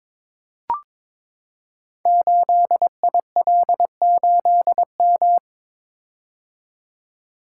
Morse code, heard as a keyed tone, about 22 wpm, 700 Hz. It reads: 8IL8M
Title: Code Groups - 5 Letter:Numbers - 22wpm